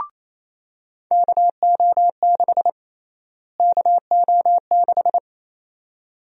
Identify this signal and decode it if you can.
Morse code, heard as a keyed tone, about 28 wpm, 700 Hz. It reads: KO6 KO6
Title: USA Call Sign Prefixes - Repeated 2x - 28wpm